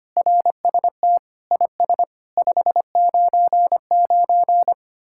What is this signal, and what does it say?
Morse code, heard as a keyed tone, about 31 wpm, 700 Hz. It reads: RST IS 599
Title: Speed-Racing Common QSO Phrases Encore - 25wpm